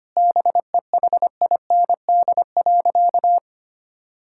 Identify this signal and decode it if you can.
Morse code, heard as a keyed tone, about 25 wpm, 700 Hz. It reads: BEHIND.
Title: Sentences from Top 1600 Words 25wpm